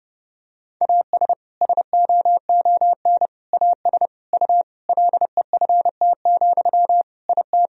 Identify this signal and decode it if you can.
Morse code, heard as a keyed tone, about 30 wpm, 700 Hz. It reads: AS SOON AS U LEFT, IT
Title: Sentences from Top 400 Words and Abbreviations 30wpm